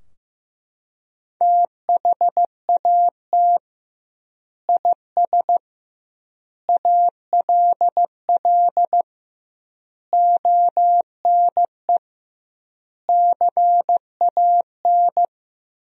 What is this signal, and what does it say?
Morse code, heard as a keyed tone, about 15 wpm, 700 Hz. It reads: THAT IS ALL ONE CAN
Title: 2x Word-Spacing - Sentences from Top 200 Words - 15wpm